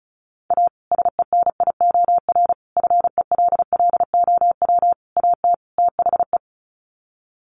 Morse code, heard as a keyed tone, about 34 wpm, 700 Hz. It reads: A SENIOR FELLOW AT THE
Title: Sets of 5 Words Encore 35wpm